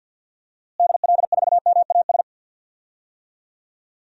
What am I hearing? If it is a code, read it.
Morse code, heard as a keyed tone, about 51 wpm, 700 Hz. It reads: DL4MAS
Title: DXCC-100 Call Signs - 6 Characters - Repeated 3x - 50wpm